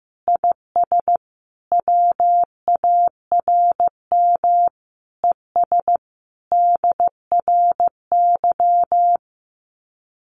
Morse code, heard as a keyed tone, about 15 wpm, 700 Hz. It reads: IS WARM ES DRY
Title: Common QSO Phrases 15wpm